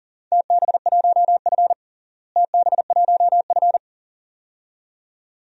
Morse code, heard as a keyed tone, about 40 wpm, 700 Hz. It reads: TB1F TB1F
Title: DXCC-100 Call Signs - 4 Characters - Repeated 2x - 40wpm